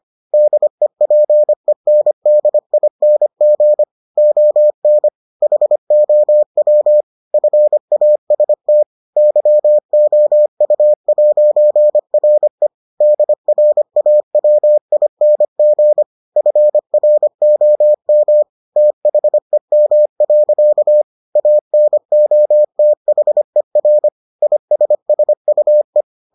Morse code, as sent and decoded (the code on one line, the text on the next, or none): -.. .--. -. -.. .. -. --. / --- -. / .... --- .-- / ..-. .- ... - / -.-- --- ..- .----. .-. / -.. .-. .- .-- .. -. --. / ..-. .-. --- -- / - .... -- .-.-.- / .- -. --- - .... .-. / .. ... ... ..- .
DEPENDING ON HOW FAST YOU'RE DRAWING FROM THEM. ANOTHER ISSUE